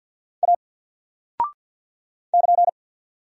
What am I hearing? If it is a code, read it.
Morse code, heard as a keyed tone, about 48 wpm, 700 Hz. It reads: A <KN>
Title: Single Character (Letter:Number:Punctuation:Prosign) 50wpm